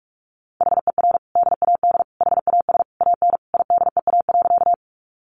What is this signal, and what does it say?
Morse code, heard as a keyed tone, about 44 wpm, 700 Hz. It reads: HER DAD HAS AN IDEA.
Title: Easy Phrases 45wpm